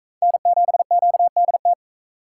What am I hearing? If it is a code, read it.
Morse code, heard as a keyed tone, about 40 wpm, 700 Hz. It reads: N7QDT